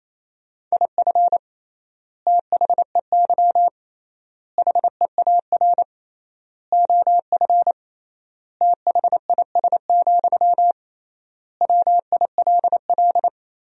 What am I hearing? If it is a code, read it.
Morse code, heard as a keyed tone, about 28 wpm, 700 Hz. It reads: IF THEY HEAR OF THIS, WILL
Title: 3x Word-Spacing - Sentences from Top 400 Words - 28wpm